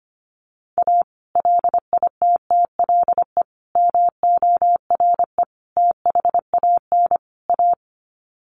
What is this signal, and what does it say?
Morse code, heard as a keyed tone, about 25 wpm, 700 Hz. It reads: A LITTLE MORE THAN A
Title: Sets of 5 Words Encore2 25wpm